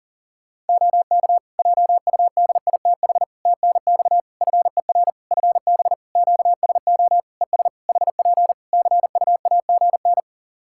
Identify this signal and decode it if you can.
Morse code, heard as a keyed tone, about 40 wpm, 700 Hz. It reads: OK JUDITH TNX FER FB QSO ES HP CUAGN
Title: Practice QSO - Exchange 4 - Ending - 40wpm